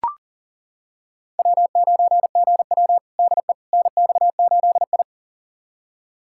Morse code, heard as a keyed tone, about 40 wpm, 700 Hz. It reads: W9GW DE NX8I
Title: USA Callsign DE Callsign - 4 Characters - 40wpm